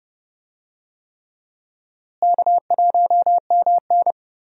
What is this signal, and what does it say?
Morse code, heard as a keyed tone, about 30 wpm, 700 Hz. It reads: K1MN
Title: USA Call Signs - 4 Characters - Repeated 3x - 30wpm